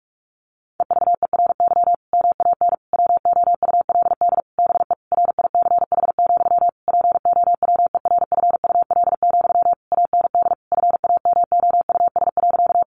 Morse code, heard as a keyed tone, about 44 wpm, 700 Hz. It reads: EVERY MAN WOULD BE RICH, POWERFUL, AND FAMOUS.
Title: Sentences from Top 1500 Words 45wpm